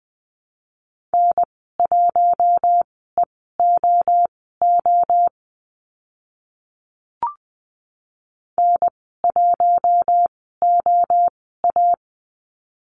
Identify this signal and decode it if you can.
Morse code, heard as a keyed tone, about 20 wpm, spaced as for 14 wpm, 700 Hz. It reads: N1EOO N1OA
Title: 048 Lesson 9g - Number 4 - Call Signs 20wpm